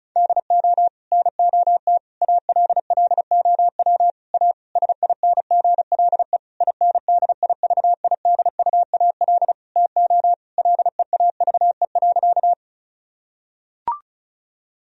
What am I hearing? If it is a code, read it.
Morse code, heard as a keyed tone, about 34 wpm, 700 Hz. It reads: DO NOT ALLOW A SINGLE INDIVIDUAL TO LEAVE.
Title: Sentences from Top 1000 Words - Review Part 4 - 35wpm